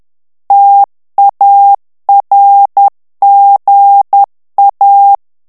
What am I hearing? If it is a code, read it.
Morse code, heard as a keyed tone, about 10 wpm, 800 Hz. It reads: TARGA
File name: ITALIEN texte clair 2 vitesse 10 mots minute